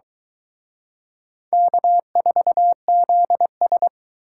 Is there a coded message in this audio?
Yes: Morse code from 1.5 to 3.9 s, heard as a keyed tone, about 23 wpm, 700 Hz. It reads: K4ZS